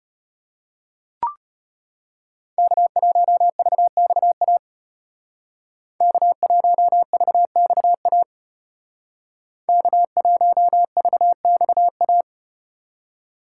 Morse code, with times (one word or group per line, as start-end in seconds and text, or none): 2.6-4.6 K1VXA
6.0-8.2 K1VXA
9.7-12.2 K1VXA